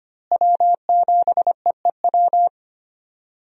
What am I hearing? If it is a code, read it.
Morse code, heard as a keyed tone, about 25 wpm, 700 Hz. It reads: W7EEW
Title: USA Call Signs - 5 Characters - Repeated 2x - 25wpm